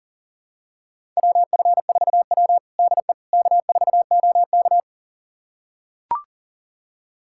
Morse code, heard as a keyed tone, about 40 wpm, 700 Hz. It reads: WF4W DE K4OK
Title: USA Callsign DE Callsign - 4 Characters - 40wpm